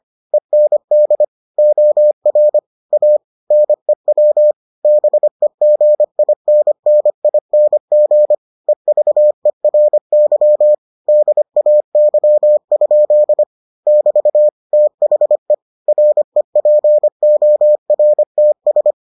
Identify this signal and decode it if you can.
Morse code, heard as a keyed tone, about 25 wpm, 600 Hz. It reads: END OR A NEW BEGINNING EVERY DAY? <BT> THE REPORTS